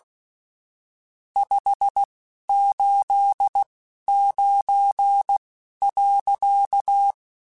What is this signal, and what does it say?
Morse code, heard as a keyed tone, about 16 wpm, spaced as for 11 wpm, 800 Hz. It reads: 589.
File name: (spaced) / QSOs Spaced-mp3s / WD0FBT 1 @15 sp